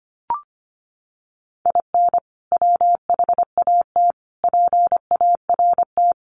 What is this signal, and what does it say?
Morse code, heard as a keyed tone, about 25 wpm, 700 Hz. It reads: IN WHAT PART
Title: Sentences from Top 200 Words 25wpm